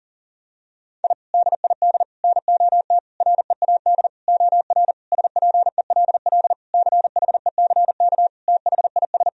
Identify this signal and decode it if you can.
Morse code, heard as a keyed tone, about 37 wpm, 700 Hz. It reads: I DID NOT READ OR SPELL CHECK THIS
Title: Sentences from Top 1000 Words - Review Part 3 - 40wpm